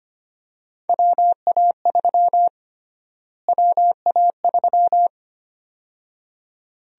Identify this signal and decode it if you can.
Morse code, heard as a keyed tone, about 25 wpm, 700 Hz. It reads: WA3 WA3
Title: USA Call Sign Prefixes - Repeated 2x - 25wpm